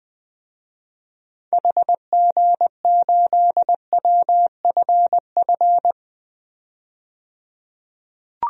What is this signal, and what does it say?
Morse code, heard as a keyed tone, about 20 wpm, 700 Hz. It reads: HG8WFF
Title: DXCC-100 Call Signs - 6 Characters - 20wpm